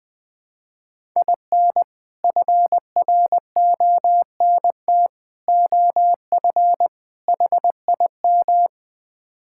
Morse code, heard as a keyed tone, about 21 wpm, 700 Hz. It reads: IN FRONT OF HIM
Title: Build A Phrase - 4-Word Phrases - 20wpm